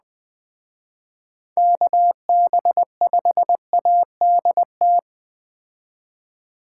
Morse code, heard as a keyed tone, about 20 wpm, 700 Hz. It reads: KB5ADT